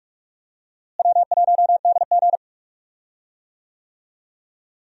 Morse code, heard as a keyed tone, about 46 wpm, 700 Hz. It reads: W1DG
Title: USA Call Signs - 4 Characters - Repeated 3x - 45wpm